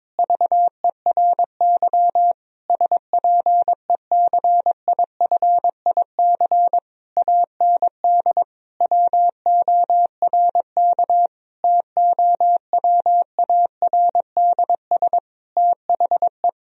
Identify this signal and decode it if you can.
Morse code, heard as a keyed tone, about 22 wpm, 700 Hz. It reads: VERY SPECIFIC AND WORK TOWARDS THE
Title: Sentences from Top 1000 Words - Review Part 3 - 22wpm